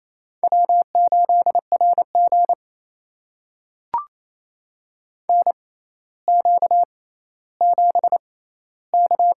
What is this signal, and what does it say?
Morse code, heard as a keyed tone, about 28 wpm, spaced as for 14 wpm, 700 Hz. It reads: W8RG NQ7K
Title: USA Call Signs - 6x Character Spacing - 28wpm